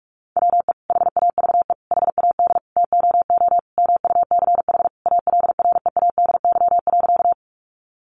Morse code, heard as a keyed tone, about 44 wpm, 700 Hz. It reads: WE HAVE HAD TOO MUCH ALREADY.
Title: Sentences from Top 700 Words 45wpm